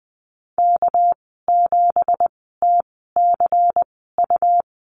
Morse code, heard as a keyed tone, about 20 wpm, spaced as for 14 wpm, 700 Hz. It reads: K7TCU
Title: 245 Lesson 39g - Letter z - Call Signs 20wpm